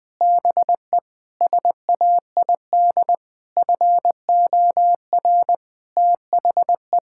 Morse code, heard as a keyed tone, about 20 wpm, 700 Hz. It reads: BE SAID FOR THE
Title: Sentences with 8-Word Phrases - 20wpm